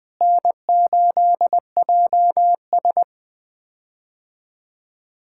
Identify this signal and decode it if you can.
Morse code, heard as a keyed tone, about 20 wpm, 700 Hz. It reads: N8JS
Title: K1USN SST QSO Practice - 20wpm